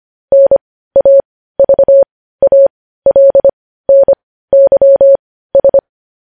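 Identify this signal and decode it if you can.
Morse code, heard as a keyed tone, about 25 wpm, spaced as for 15 wpm, 550 Hz. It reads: NAVALNYS
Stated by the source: Morse code news headlines practice at 15, preceded by vvv.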